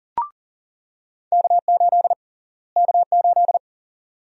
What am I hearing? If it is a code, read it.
Morse code, heard as a keyed tone, about 40 wpm, 700 Hz. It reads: K8 K8
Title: USA Call Sign Prefixes - Repeated 2x - 40wpm